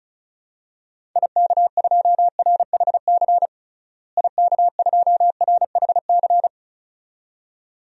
Morse code, heard as a keyed tone, about 34 wpm, 700 Hz. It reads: IK2RHC IK2RHC
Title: DXCC-100 Call Signs - 6 Characters - Repeated 2x - 35wpm